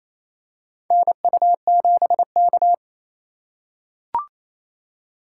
Morse code, heard as a keyed tone, about 28 wpm, 700 Hz. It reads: NU7K
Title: USA Call Signs - 3x Character Spacing - 28wpm